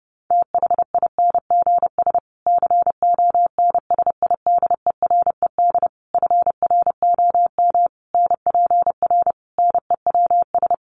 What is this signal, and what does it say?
Morse code, heard as a keyed tone, about 30 wpm, 700 Hz. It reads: THINGS CONSIDERED FROM NPR NEWS
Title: Sets of 5 Words Encore 30wpm